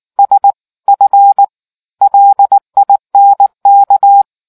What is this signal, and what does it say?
Morse code, heard as a keyed tone, about 18 wpm, 800 Hz. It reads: SF LINK